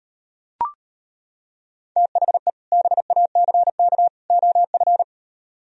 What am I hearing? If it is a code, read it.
Morse code, heard as a keyed tone, about 34 wpm, 700 Hz. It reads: THE BACK OF